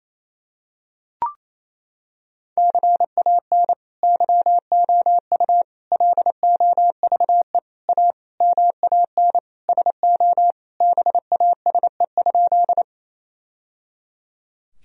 Morse code, heard as a keyed tone, about 28 wpm, 700 Hz. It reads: CAN YOU LOVE A MAN SO BASE?
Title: Sentences from Top 1000 Words - Review Part 1 - 28wpm